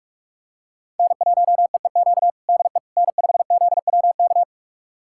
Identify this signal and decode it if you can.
Morse code, heard as a keyed tone, about 45 wpm, 700 Hz. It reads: N1EEQ DE N5ZWK
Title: USA Callsign DE Callsign - 5 Characters - 45wpm